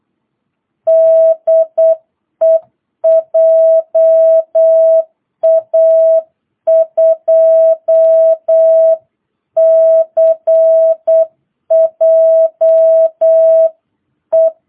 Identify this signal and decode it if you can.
Morse code, heard as a keyed tone, about 8 wpm, 660 Hz. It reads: DE JA2CJE